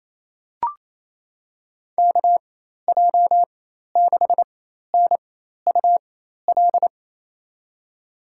Morse code, heard as a keyed tone, about 28 wpm, spaced as for 13 wpm, 700 Hz. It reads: KJ6NUL
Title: USA Call Signs - 6 Characters - 4x Character Spacing - 28wpm